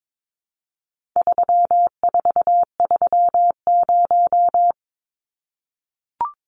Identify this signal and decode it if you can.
Morse code, heard as a keyed tone, about 22 wpm, 700 Hz. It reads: 3430
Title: Numbers - 4-Digit 22wpm